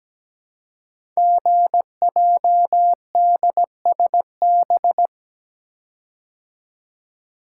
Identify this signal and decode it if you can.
Morse code, heard as a keyed tone, about 17 wpm, 700 Hz. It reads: GJDSB
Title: Code Groups - 5 Letters - 17wpm